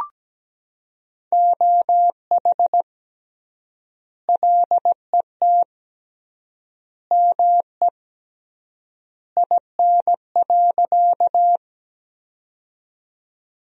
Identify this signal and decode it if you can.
Morse code, heard as a keyed tone, about 17 wpm, 700 Hz. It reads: OH LET ME IN.
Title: 3x Word-Spacing - Sentences from Top 500 Words - 17wpm